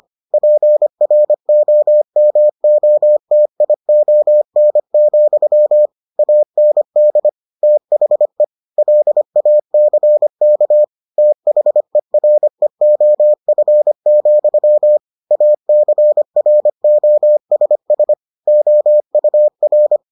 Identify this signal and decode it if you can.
Morse code, heard as a keyed tone, about 25 wpm, 600 Hz. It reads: PROMOTION, AND THE LACK THEREOF, ACROSS OUR